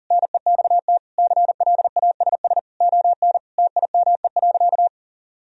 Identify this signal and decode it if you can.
Morse code, heard as a keyed tone, about 40 wpm, 700 Hz. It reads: NEXT CLASS ON TIME.
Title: Sentences from Top 1000 Words - Review Part 2 - 40wpm